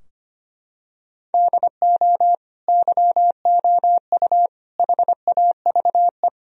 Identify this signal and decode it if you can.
Morse code, heard as a keyed tone, about 25 wpm, 700 Hz. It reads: DO YOU HAVE
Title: Sentences from Top 1000 Words - Review Part 4 - 25wpm